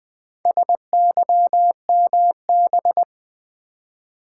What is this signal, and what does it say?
Morse code, heard as a keyed tone, about 20 wpm, 700 Hz. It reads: SYMB
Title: Build A Word - 2 Letters At A Time - 8-Letter Words - 20wpm